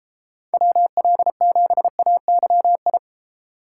Morse code, heard as a keyed tone, about 31 wpm, 700 Hz. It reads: WL7AYI